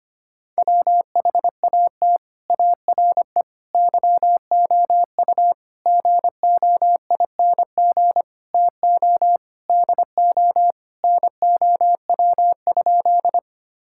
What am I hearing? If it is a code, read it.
Morse code, heard as a keyed tone, about 25 wpm, 700 Hz. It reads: WHAT ARE YOU GOING TO DO NOW?
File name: practice / 25wpm / Sentences with 3-Word Phrases - 25wpm